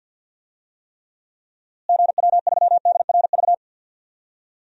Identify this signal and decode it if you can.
Morse code, heard as a keyed tone, about 50 wpm, 700 Hz. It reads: GW3DRV
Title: DXCC-100 Call Signs - 6 Characters - Repeated 3x - 50wpm